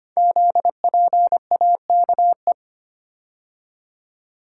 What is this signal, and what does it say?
Morse code, heard as a keyed tone, about 25 wpm, 700 Hz. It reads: ZPAKE
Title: Code Groups - 5 Letter:Numbers - 25wpm